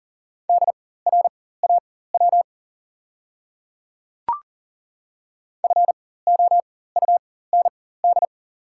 Morse code, heard as a keyed tone, about 40 wpm, spaced as for 19 wpm, 690 Hz. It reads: DRAW FOUND
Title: 4x Character Spacing - Top 300 Words - 40wpm